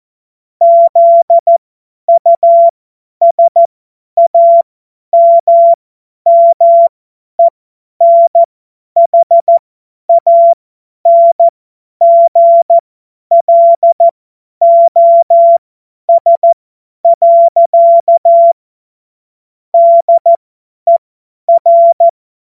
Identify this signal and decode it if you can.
Morse code, heard as a keyed tone, about 14 wpm, spaced as for 10 wpm, 680 Hz. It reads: ZUSAMMENHANGLOS. DER
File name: Morsekurs / Witze / Witz 14. Juni 2022 50 F70